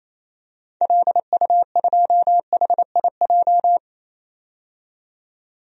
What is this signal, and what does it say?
Morse code, heard as a keyed tone, about 28 wpm, 700 Hz. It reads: LU2HIJ
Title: DXCC-100 Call Signs - 6 Characters - 28wpm